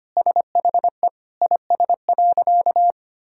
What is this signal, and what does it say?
Morse code, heard as a keyed tone, about 25 wpm, 700 Hz. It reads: SHE IS.
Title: Sentences from Top 1000 Words - Review Part 6 - 25wpm